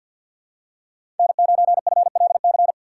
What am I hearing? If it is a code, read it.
Morse code, heard as a keyed tone, about 51 wpm, 700 Hz. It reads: N9FLC
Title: USA Call Signs - 2x Character Spacing - 50wpm